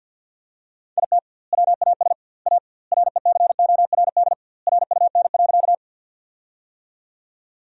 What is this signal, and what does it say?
Morse code, heard as a keyed tone, about 46 wpm, 700 Hz. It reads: IT WAS A RECORD RUN.
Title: 2x Word-Spacing - Sentences from Top 500 Words - 50wpm